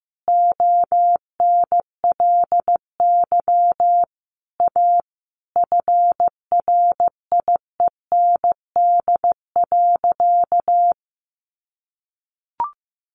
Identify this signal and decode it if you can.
Morse code, heard as a keyed tone, about 15 wpm, 700 Hz. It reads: ONLY A FRIEND.
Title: Sentences from Top 1000 Words 15wpm